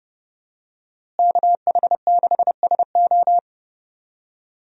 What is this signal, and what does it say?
Morse code, heard as a keyed tone, about 30 wpm, 700 Hz. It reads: KH6SO
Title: USA Call Signs - 5 Characters - Repeated 3x - 30wpm